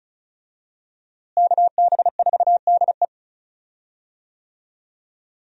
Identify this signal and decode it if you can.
Morse code, heard as a keyed tone, about 36 wpm, 700 Hz. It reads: KB4DE
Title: USA Call Signs - 5 Characters - Repeated 3x - 35wpm